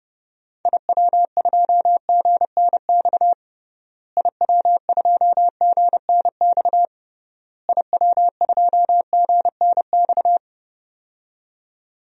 Morse code, heard as a keyed tone, about 32 wpm, 700 Hz. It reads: IW2GNX IW2GNX IW2GNX
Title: DXCC-100 Call Signs - 6 Characters - Repeated 3x - 30wpm